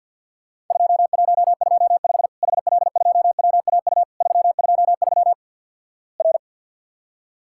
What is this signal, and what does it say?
Morse code, heard as a keyed tone, about 50 wpm, 700 Hz. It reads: 2125 HL2WAU 323 R
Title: Worked All Europe DX Contest - QTC Exchange - 50wpm